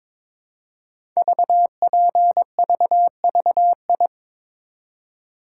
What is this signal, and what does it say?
Morse code, heard as a keyed tone, about 22 wpm, 700 Hz. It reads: VPVVI
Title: Code Groups - 5 Letters - 22wpm